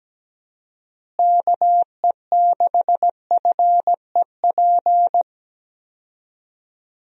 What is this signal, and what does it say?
Morse code, heard as a keyed tone, about 17 wpm, 700 Hz. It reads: KE6FEP
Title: USA Call Signs - 6 Characters - Repeated 2x - 17wpm